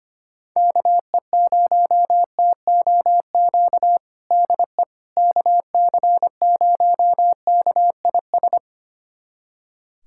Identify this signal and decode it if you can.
Morse code, heard as a keyed tone, about 25 wpm, 700 Hz. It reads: KE0TOQ DE KC0KIS
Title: USA Callsign DE Callsign - 6 Characters - 25wpm